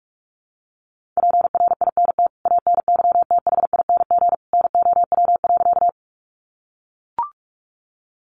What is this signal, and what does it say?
Morse code, heard as a keyed tone, about 44 wpm, 700 Hz. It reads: PRINT ANYTHING NOW.
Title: Sentences from Top 1400 Words 45wpm